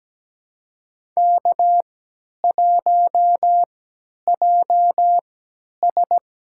Morse code, heard as a keyed tone, about 17 wpm, spaced as for 10 wpm, 700 Hz. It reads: K1JS
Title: USA Call Signs - 3x Character Spacing - 17wpm